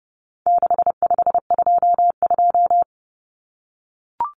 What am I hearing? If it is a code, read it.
Morse code, heard as a keyed tone, about 30 wpm, 700 Hz. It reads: 6522
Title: Common Cut Numbers - 4-Digit 30wpm